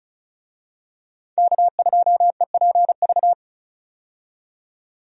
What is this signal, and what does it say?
Morse code, heard as a keyed tone, about 36 wpm, 700 Hz. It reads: K2EPV
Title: USA Call Signs - 5 Characters - 35wpm